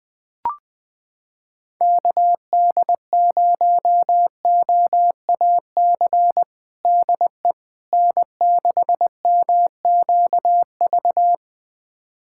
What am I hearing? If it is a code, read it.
Morse code, heard as a keyed tone, about 20 wpm, 700 Hz. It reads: KD0OAC DE N6MQV
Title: USA Callsign DE Callsign 20wpm